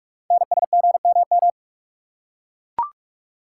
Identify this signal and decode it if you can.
Morse code, heard as a keyed tone, about 44 wpm, 700 Hz. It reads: NSGMM
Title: Code Groups - 5 Letters - 45wpm